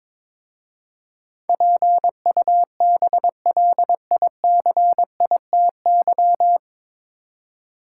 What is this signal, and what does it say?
Morse code, heard as a keyed tone, about 22 wpm, 700 Hz. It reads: PUBLICITY
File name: 9 Letter Words - First 3 Letters Given - 22wpm